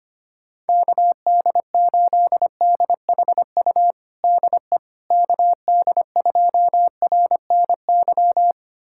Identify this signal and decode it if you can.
Morse code, heard as a keyed tone, about 25 wpm, 700 Hz. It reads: KD8DHU DE KD2RNY
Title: USA Callsign DE Callsign 25wpm